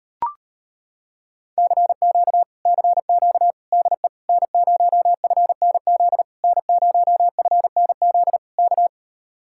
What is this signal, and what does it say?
Morse code, heard as a keyed tone, about 33 wpm, 700 Hz. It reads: CQ CQ DE N0FNZ N0FNZ K